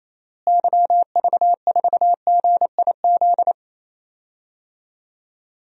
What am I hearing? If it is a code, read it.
Morse code, heard as a keyed tone, about 28 wpm, 700 Hz. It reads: YV4GIZ
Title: DXCC-100 Call Signs - 6 Characters - Repeated 2x - 28wpm